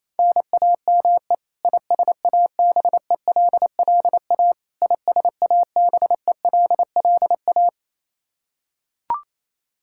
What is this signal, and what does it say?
Morse code, heard as a keyed tone, about 28 wpm, 700 Hz. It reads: NAME ISABELLA ISABELLA
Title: Practice QSO - Exchange 1 - Beginning - 28wpm